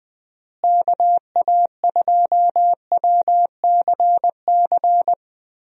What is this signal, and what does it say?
Morse code, heard as a keyed tone, about 20 wpm, 700 Hz. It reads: KA2WCC
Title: USA Call Signs - 6 Characters - Repeated 3x - 20wpm